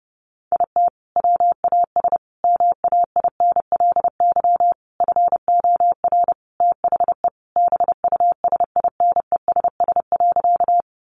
Morse code, heard as a keyed tone, about 30 wpm, 700 Hz. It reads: IT WAS MAINLY FOR THE BUSINESS.
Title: Sentences from Top 1400 Words 30wpm